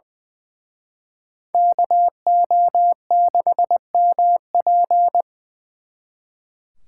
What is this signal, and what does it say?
Morse code, heard as a keyed tone, about 24 wpm, 700 Hz. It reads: KO6MP
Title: Speed Racing - USA 2x2 Call Signs 20wpm